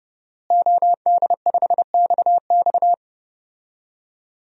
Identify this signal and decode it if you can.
Morse code, heard as a keyed tone, about 30 wpm, 700 Hz. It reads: OD5XX
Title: DXCC-100 Call Signs - 5 Characters - Repeated 2x - 30wpm